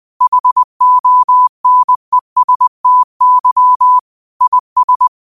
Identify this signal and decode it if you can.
Morse code, heard as a keyed tone, about 20 wpm, 1000 Hz. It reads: HONESTY IS
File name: qotdfiles / 20wpm